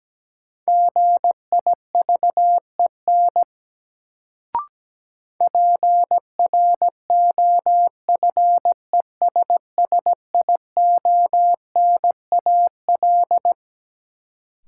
Morse code, heard as a keyed tone, about 17 wpm, 700 Hz. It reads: GIVEN PROFESSIONAL
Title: Sentences from Top 1400 Words - New Word Drill Alt - 17wpm